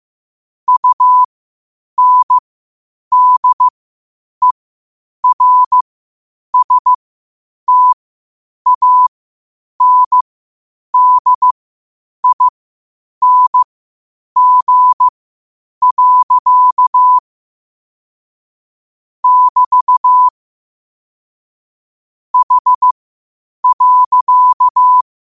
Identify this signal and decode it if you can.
Morse code, heard as a keyed tone, about 15 wpm, spaced as for 8 wpm, 1000 Hz. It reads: UNDERSTANDING. <BT> H.
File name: qotdfiles / 8wpm